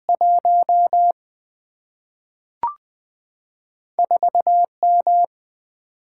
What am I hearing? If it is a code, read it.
Morse code, heard as a keyed tone, about 20 wpm, 700 Hz. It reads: 1 4 M
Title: Single-Letter Single-Number 20wpm